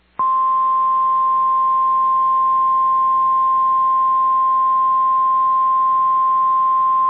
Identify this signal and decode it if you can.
Emergency Alert System (SAME) tones (the 1050 Hz attention tone). It is heard from 0.3 to 6.8 s.